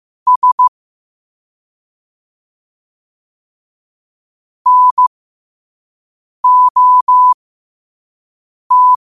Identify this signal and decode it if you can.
Morse code, heard as a keyed tone, about 15 wpm, spaced as for 5 wpm, 1000 Hz. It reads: S NOT
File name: qotdfiles / 5wpm